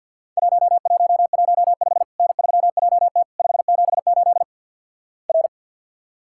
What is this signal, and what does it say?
Morse code, heard as a keyed tone, about 50 wpm, 700 Hz. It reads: 1115 N3JT 578 R
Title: Worked All Europe DX Contest - QTC Exchange - 50wpm